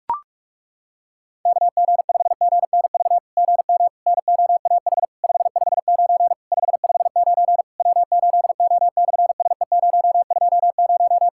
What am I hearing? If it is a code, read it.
Morse code, heard as a keyed tone, about 46 wpm, 700 Hz. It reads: KG5GNV GM NOAH 559 559 W8O/SE020
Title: SOTA QSO Practice 45wpm